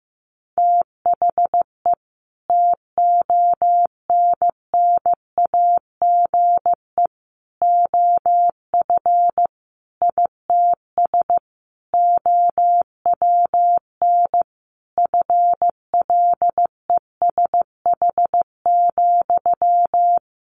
Morse code, heard as a keyed tone, about 15 wpm, 700 Hz. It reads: THE TONNAGE OF ITS OWN FLESH,
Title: A Sound Of Thunder 15wpm